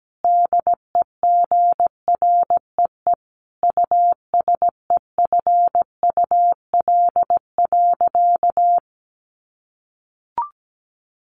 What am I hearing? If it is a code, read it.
Morse code, heard as a keyed tone, about 17 wpm, 700 Hz. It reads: DEGREE USEFUL.
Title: Sentences from Top 1500 Words 17wpm